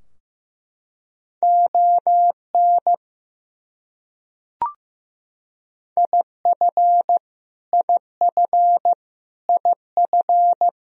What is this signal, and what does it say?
Morse code, heard as a keyed tone, about 15 wpm, 700 Hz. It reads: ON IF IF IF
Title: Top 100 Words Repeated 15wpm